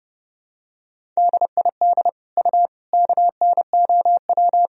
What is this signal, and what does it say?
Morse code, heard as a keyed tone, about 30 wpm, 700 Hz. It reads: DID U KNOW
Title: Sentences from Top 100 Words and Abbreviations 30wpm